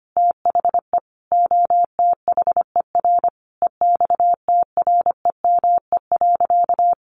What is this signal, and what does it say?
Morse code, heard as a keyed tone, about 25 wpm, 700 Hz. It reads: THE OTHER EXTREME.
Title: Sentences from Top 1300 Words 25wpm